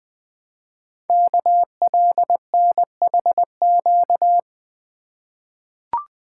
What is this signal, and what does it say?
Morse code, heard as a keyed tone, about 20 wpm, 700 Hz. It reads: KLNHQ
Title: Code Groups - 5 Letters - 20wpm